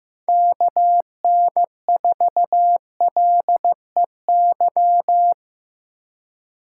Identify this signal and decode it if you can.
Morse code, heard as a keyed tone, about 18 wpm, 700 Hz. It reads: KN4LEY
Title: Speed Racing - USA 2x3 Call Signs 15wpm